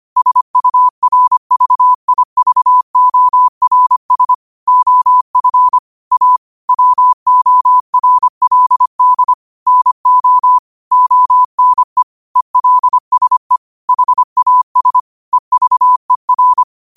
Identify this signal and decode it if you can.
Morse code, heard as a keyed tone, about 25 wpm, 1000 Hz. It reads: SURVIVORS OF A WORLD NO ONE ELSE HAS EVER
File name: qotdfiles / 25wpm